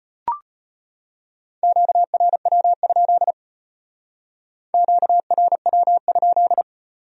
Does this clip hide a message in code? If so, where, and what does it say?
Yes: Morse code from 1.6 to 6.6 s, heard as a keyed tone, about 33 wpm, 700 Hz. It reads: QRW? QRW?